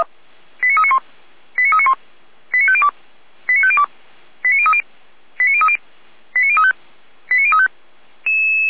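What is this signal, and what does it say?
Five-tone selective call (ZVEI), heard as quick runs of five tones, tones of about 70 ms. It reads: 89391 89391 89502 89502 89030 89030 89035 89035
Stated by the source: ZVEI five-tone selective call received on an SDR.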